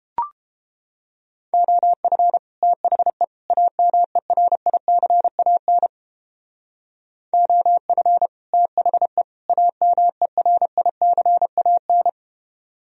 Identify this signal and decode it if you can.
Morse code, heard as a keyed tone, about 29 wpm, 700 Hz. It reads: OF THE AMERICAN OF THE AMERICAN